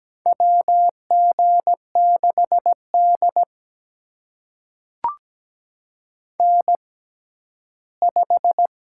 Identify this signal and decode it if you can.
Morse code, heard as a keyed tone, about 17 wpm, spaced as for 11 wpm, 700 Hz. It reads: WG6D N5
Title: USA Call Signs - 4 Characters - 6x Character Spacing - 17wpm